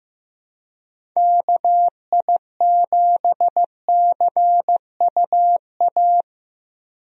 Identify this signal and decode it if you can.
Morse code, heard as a keyed tone, about 16 wpm, 700 Hz. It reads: KI7CUA
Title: USA Call Signs - 6 Characters - Repeated 2x - 15wpm